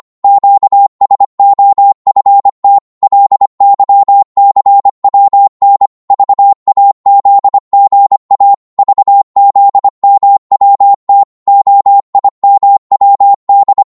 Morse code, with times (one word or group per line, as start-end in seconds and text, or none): 0.2-2.8 QSOFT
3.0-5.9 LYCWN
6.1-8.5 VAZGA
8.8-11.2 VZMWT
11.5-13.8 OIMWD